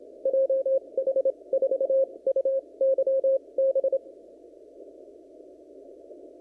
Morse code, heard as a keyed tone, about 27 wpm, spaced as for 22 wpm, 540 Hz. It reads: JH4UYB
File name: CW / kt5j wpx cw-op-k5pi